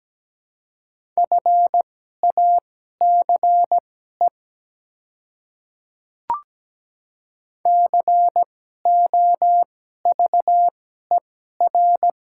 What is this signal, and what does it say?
Morse code, heard as a keyed tone, about 17 wpm, spaced as for 12 wpm, 700 Hz. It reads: FACE COVER
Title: 2x Character Spacing - Top 300 Words - 17wpm